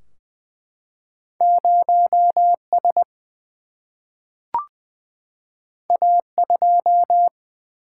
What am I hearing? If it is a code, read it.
Morse code, heard as a keyed tone, about 20 wpm, 700 Hz. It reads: 0S A2
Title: Single-Letter Single-Number 20wpm